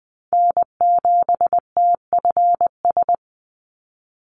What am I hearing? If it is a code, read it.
Morse code, heard as a keyed tone, about 20 wpm, 700 Hz. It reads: N7TFS
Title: USA Call Signs 20wpm